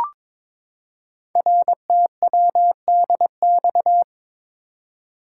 Morse code, heard as a keyed tone, about 22 wpm, 700 Hz. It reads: RTWDX
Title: Code Groups - 5 Letters - 22wpm